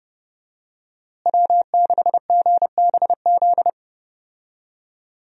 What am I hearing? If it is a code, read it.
Morse code, heard as a keyed tone, about 30 wpm, 700 Hz. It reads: W6GBZ
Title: USA Call Signs - 5 Characters - Repeated 2x - 30wpm